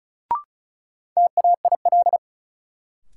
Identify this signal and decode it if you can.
Morse code, heard as a keyed tone, about 36 wpm, 700 Hz. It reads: TAIL
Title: Sentences from Top 1000 Words - Review Part 3 - New Word Drill - 35wpm